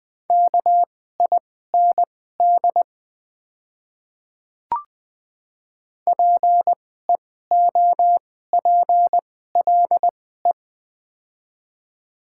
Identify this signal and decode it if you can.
Morse code, heard as a keyed tone, about 20 wpm, spaced as for 15 wpm, 700 Hz. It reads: KIND PEOPLE
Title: 2x Character Spacing - Top 200 Words - 20wpm